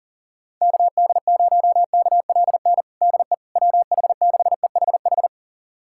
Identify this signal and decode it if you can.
Morse code, heard as a keyed tone, about 40 wpm, 700 Hz. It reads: KD0KLN DE WH6EHH
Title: USA Callsign DE Callsign - 6 Characters - 40wpm